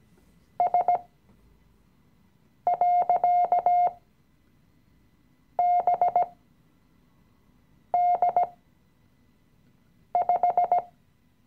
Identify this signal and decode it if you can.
Morse code, heard as a keyed tone, about 17 wpm, spaced as for 5 wpm, 710 Hz. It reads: S.BD5
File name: CW-Online-Kurs / Anfänger / Lektion 018